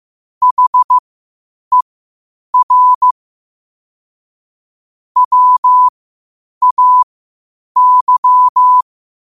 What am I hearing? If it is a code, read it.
Morse code, heard as a keyed tone, about 15 wpm, spaced as for 8 wpm, 1000 Hz. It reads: HER WAY
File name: qotdfiles / 8wpm